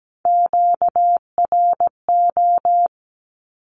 Morse code, heard as a keyed tone, about 17 wpm, 700 Hz. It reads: QRO
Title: Q-Codes 17wpm